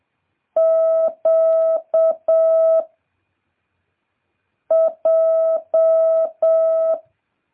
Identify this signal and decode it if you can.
Morse code, heard as a keyed tone, about 7 wpm, 650 Hz. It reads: Q J